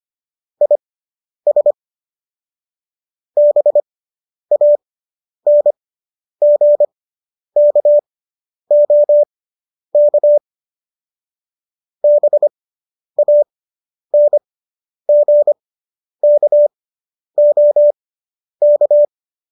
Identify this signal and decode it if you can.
Morse code, heard as a keyed tone, about 25 wpm, spaced as for 10 wpm, 590 Hz. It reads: IS BANGKOK BANGKOK